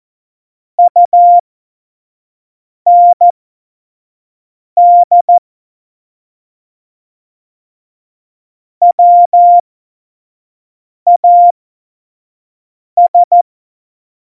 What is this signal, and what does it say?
Morse code, heard as a keyed tone, about 14 wpm, spaced as for 5 wpm, 700 Hz. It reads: UND WAS
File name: Morsekurs / Witze / Witz 24.Juli 2018, 25 F70